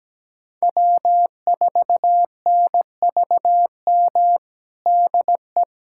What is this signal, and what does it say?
Morse code, heard as a keyed tone, about 17 wpm, 700 Hz. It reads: W4NVM DE
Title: USA Callsign DE Callsign - 5 Characters - 17wpm